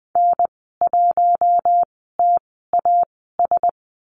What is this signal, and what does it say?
Morse code, heard as a keyed tone, about 20 wpm, spaced as for 14 wpm, 700 Hz. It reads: N1TAS
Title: 048 Lesson 9g - Number 4 - Call Signs 20wpm